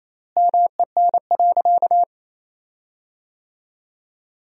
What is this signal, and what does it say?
Morse code, heard as a keyed tone, about 28 wpm, 700 Hz. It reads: MEN.
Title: 4x Word-Spacing - Sentences from Top 300 Words - 28wpm